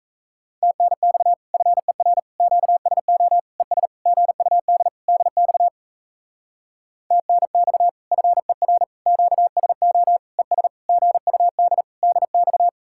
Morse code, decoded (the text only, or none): TNX FER QSO ES GUD DX TNX FER QSO ES GUD DX